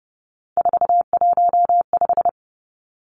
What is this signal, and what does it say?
Morse code, heard as a keyed tone, about 30 wpm, 700 Hz. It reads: 415
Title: Three Challenging Letter-Number Combinations 30wpm